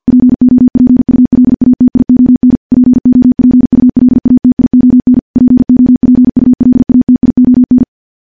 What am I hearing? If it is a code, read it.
Morse code, heard as a keyed tone, about 48 wpm, 250 Hz. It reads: POPULATION POPULATION POPULATION
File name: morse / morse-training-50WPM